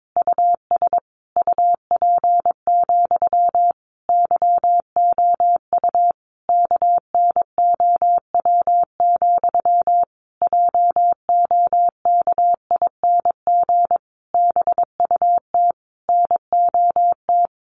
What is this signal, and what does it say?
Morse code, heard as a keyed tone, about 22 wpm, 700 Hz. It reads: US UP, YOU KNOW, JOKING BUT NOT
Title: A Sound Of Thunder 22wpm